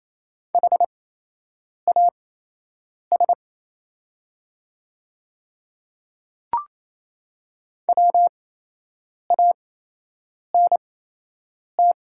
Morse code, heard as a keyed tone, about 28 wpm, spaced as for 8 wpm, 700 Hz. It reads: HAS WANT
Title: 8x Character Spacing - Top 200 Words - 28wpm